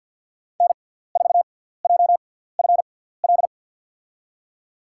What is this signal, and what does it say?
Morse code, heard as a keyed tone, about 48 wpm, spaced as for 17 wpm, 700 Hz. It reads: N4JFL
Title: USA Call Signs - 6x Character Spacing - 50wpm